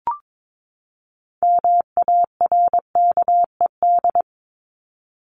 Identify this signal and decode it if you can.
Morse code, heard as a keyed tone, about 22 wpm, 700 Hz. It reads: MARKED
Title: ED Suffix - Top 100 Words with Base Words 22wpm